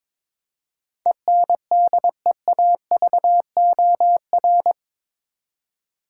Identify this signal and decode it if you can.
Morse code, heard as a keyed tone, about 21 wpm, 700 Hz. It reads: ENDEAVOR
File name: practice / 22wpm / Eight Letter Words - 22wpm